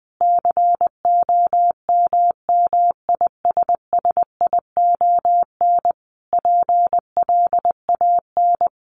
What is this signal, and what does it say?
Morse code, heard as a keyed tone, about 20 wpm, 700 Hz. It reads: COMMISSION PLAN
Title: Sentences from Top 1400 Words 20wpm